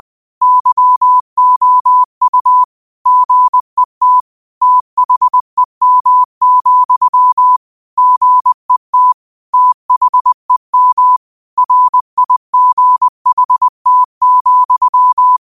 Morse code, heard as a keyed tone, about 20 wpm, 1000 Hz. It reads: YOU GET THEM, GET THEM RIGHT,
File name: qotdfiles / 20wpm